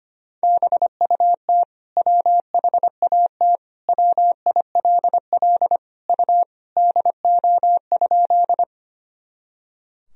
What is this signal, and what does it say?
Morse code, heard as a keyed tone, about 25 wpm, 700 Hz. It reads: BUT WHAT WILL U DO?
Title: Sentences from Top 100 Words and Abbreviations 25wpm